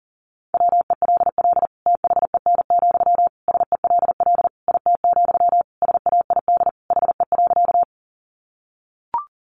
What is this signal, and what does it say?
Morse code, heard as a keyed tone, about 40 wpm, 700 Hz. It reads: WELL THEN, SELL IT, SAID HE.
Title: Sentences from Top 1100 Words 40wpm